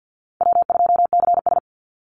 Morse code, heard as a keyed tone, about 50 wpm, 700 Hz. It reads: W2CS
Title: Easy Call Signs Repeated 2x - 50wpm